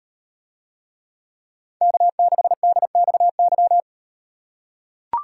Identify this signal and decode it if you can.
Morse code, heard as a keyed tone, about 38 wpm, 700 Hz. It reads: K6DXY